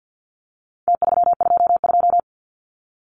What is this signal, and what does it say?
Morse code, heard as a keyed tone, about 49 wpm, 700 Hz. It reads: T322
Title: Numbers - Contest Style - 4-Digit 50wpm